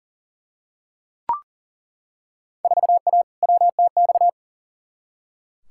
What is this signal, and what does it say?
Morse code, heard as a keyed tone, about 40 wpm, 700 Hz. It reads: 4A WTX
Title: Field Day Exchanges - 40wpm